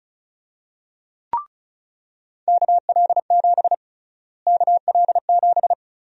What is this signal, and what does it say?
Morse code, heard as a keyed tone, about 36 wpm, 700 Hz. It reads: KL7 KL7
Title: USA Call Sign Prefixes - Repeated 2x - 35wpm